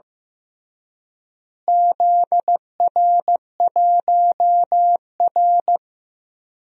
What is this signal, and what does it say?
Morse code, heard as a keyed tone, about 15 wpm, 700 Hz. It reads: ZR1R
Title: DXCC-100 Call Signs - 4 Characters - Repeated 3x - 15wpm